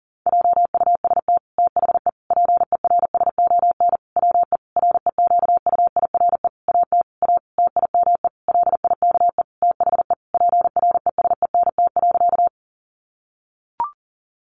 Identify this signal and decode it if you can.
Morse code, heard as a keyed tone, about 40 wpm, 700 Hz. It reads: JUST THE PERSON WE REQUIRE AT A TIME LIKE THE PRESENT.
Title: Sentences from Top 900 Words 40wpm